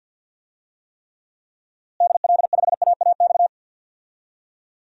Morse code, heard as a keyed tone, about 47 wpm, 700 Hz. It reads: DL5AAX
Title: DXCC-100 Call Signs - 6 Characters - Repeated 2x - 50wpm